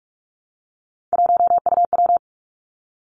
Morse code, heard as a keyed tone, about 46 wpm, 700 Hz. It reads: 1UW
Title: Three Challenging Letter-Number Combinations 45wpm